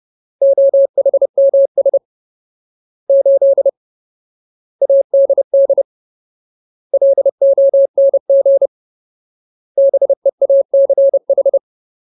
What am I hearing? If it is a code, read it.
Morse code, heard as a keyed tone, about 30 wpm, 550 Hz. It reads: OHMS 8 ADD LONG BEACH
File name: Ultimate Word List / 30WPM Ultimate